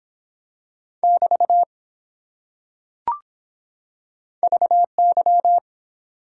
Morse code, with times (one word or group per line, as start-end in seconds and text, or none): 1.0-1.6 <BT>
4.4-5.6 VY